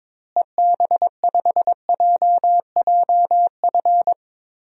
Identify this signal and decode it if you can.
Morse code, heard as a keyed tone, about 22 wpm, 700 Hz. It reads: EB5JJF
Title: DXCC-100 Call Signs - 6 Characters - Repeated 2x - 22wpm